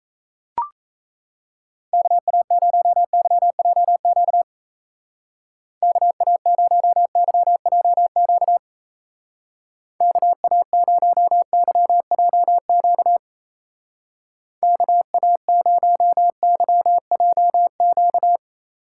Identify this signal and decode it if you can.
Morse code, heard as a keyed tone, about 36 wpm, 700 Hz. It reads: KA0YJQ KA0YJQ KA0YJQ KA0YJQ